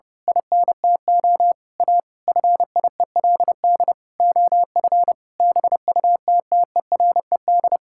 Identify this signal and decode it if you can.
Morse code, heard as a keyed tone, about 30 wpm, 700 Hz. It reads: INTO A FIELD OF BUTTERED
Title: Pangram Challenge v2 - 30wpm